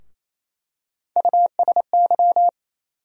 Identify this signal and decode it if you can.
Morse code, heard as a keyed tone, about 28 wpm, 700 Hz. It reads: USY
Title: USA Call Sign Postfixes - 3 Letters 28wpm